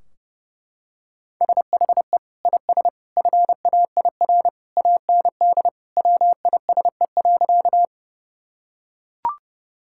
Morse code, heard as a keyed tone, about 30 wpm, 700 Hz. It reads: SHE IS FAIR AND WISE.
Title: Sentences from Top 1600 Words 30wpm